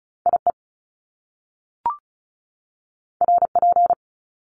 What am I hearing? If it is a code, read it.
Morse code, heard as a keyed tone, about 36 wpm, 700 Hz. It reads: IE RP
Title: Top Two Letter Combinations in Words 35wpm